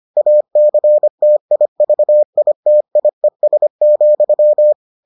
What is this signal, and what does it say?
Morse code, heard as a keyed tone, about 25 wpm, 600 Hz. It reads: ACTIVITIES,